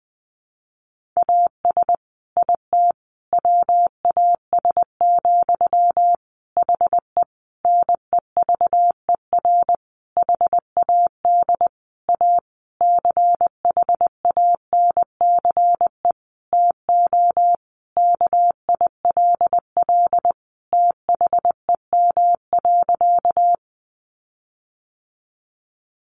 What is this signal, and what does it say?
Morse code, heard as a keyed tone, about 20 wpm, 700 Hz. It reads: AS IT WAS, HE NEVER HAD A CHANCE TO KILL THEM.
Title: Sentences from Top 1400 Words 20wpm